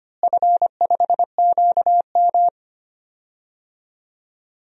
Morse code, heard as a keyed tone, about 25 wpm, 700 Hz. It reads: F5QM
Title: DXCC-100 Call Signs - 4 Characters - Repeated 2x - 25wpm